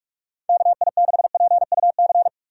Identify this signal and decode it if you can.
Morse code, heard as a keyed tone, about 41 wpm, 690 Hz. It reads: KI6PUC
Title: USA Call Signs - 6 Characters - 3x Character Spacing - 45wpm